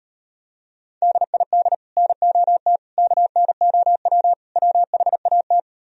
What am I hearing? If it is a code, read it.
Morse code, heard as a keyed tone, about 38 wpm, 700 Hz. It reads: DID NOT KNOW WHAT